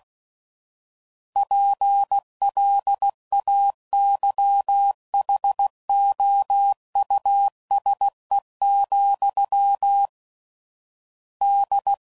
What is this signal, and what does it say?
Morse code, heard as a keyed tone, about 16 wpm, 800 Hz. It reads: PLAYHOUSE, D
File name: Suffixes / General / Houses